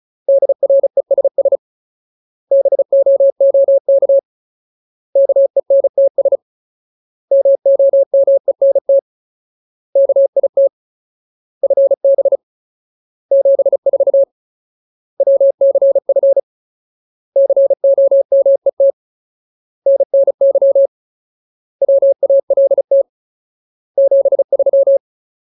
Morse code, heard as a keyed tone, about 34 wpm, 550 Hz. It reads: DRESS BOOK KENTS MOMENT KIT FB 74 WCF COMET NNY WALT 73
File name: Ultimate Word List / 35WPM Ultimate